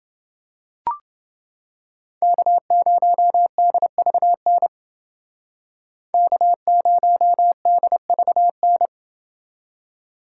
Morse code, heard as a keyed tone, about 30 wpm, 700 Hz. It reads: K0DVN K0DVN